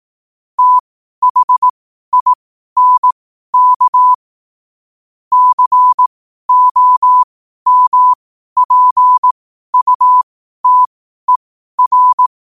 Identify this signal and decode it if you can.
Morse code, heard as a keyed tone, about 18 wpm, spaced as for 12 wpm, 1000 Hz. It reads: THINK COMPUTER
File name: qotdfiles / 12wpm